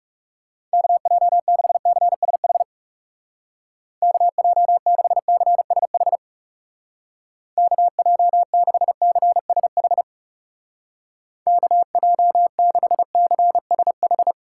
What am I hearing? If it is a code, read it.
Morse code, heard as a keyed tone, about 37 wpm, 700 Hz. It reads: KJ6CSH KJ6CSH KJ6CSH KJ6CSH
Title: Speed Racing - USA 2x3 Call Signs 30wpm